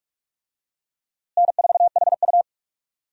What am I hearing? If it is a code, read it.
Morse code, heard as a keyed tone, about 46 wpm, 700 Hz. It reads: N4HU
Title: USA Call Signs - 4 Characters - Repeated 3x - 45wpm